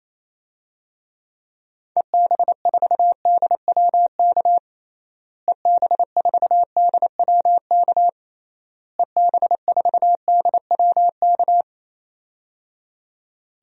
Morse code, heard as a keyed tone, about 27 wpm, 700 Hz. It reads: EB4DWK EB4DWK EB4DWK
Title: DXCC-100 Call Signs - 6 Characters - Repeated 3x - 28wpm